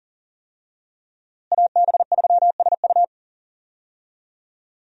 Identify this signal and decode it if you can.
Morse code, heard as a keyed tone, about 40 wpm, 700 Hz. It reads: AB3SU
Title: USA Call Signs - 5 Characters - Repeated 3x - 40wpm